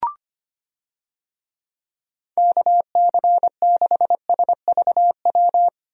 Morse code, heard as a keyed tone, about 25 wpm, 700 Hz. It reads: KC6SVW
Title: USA Call Signs - 6 Characters - Repeated 2x - 25wpm